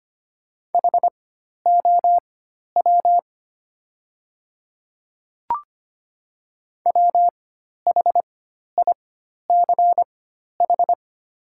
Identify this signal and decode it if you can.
Morse code, heard as a keyed tone, about 25 wpm, spaced as for 12 wpm, 700 Hz. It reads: HOW WHICH
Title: 4x Character Spacing - Top 100 Words - 25wpm